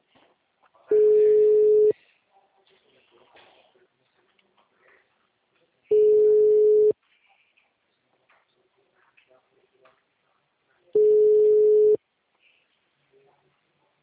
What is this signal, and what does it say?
Ringback tone (Europe, 425 Hz), with speech.
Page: Braganca - SP